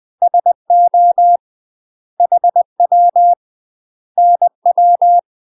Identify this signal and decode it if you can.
Morse code, heard as a keyed tone, about 20 wpm, 700 Hz. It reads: SO HW NW
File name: UK MASTER QSOs TRAINING / 20WPM / UK MASTER QSOs 0002